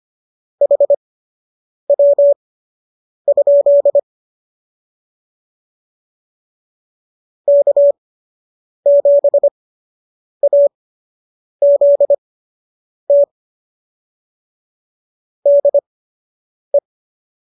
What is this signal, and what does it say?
Morse code, heard as a keyed tone, about 25 wpm, spaced as for 8 wpm, 570 Hz. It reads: HW? K7AZT DE
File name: qso006 08